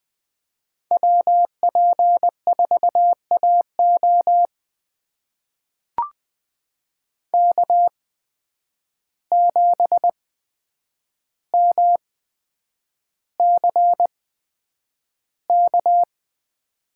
Morse code, heard as a keyed tone, about 20 wpm, spaced as for 9 wpm, 700 Hz. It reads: WP4AO K7MCK
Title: USA Call Signs - 5 Characters - 8x Character Spacing - 20wpm